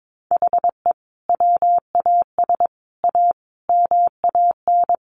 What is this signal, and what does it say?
Morse code, heard as a keyed tone, about 22 wpm, 700 Hz. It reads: HE WAS A MAN
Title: Sentences from Top 1200 Words 22wpm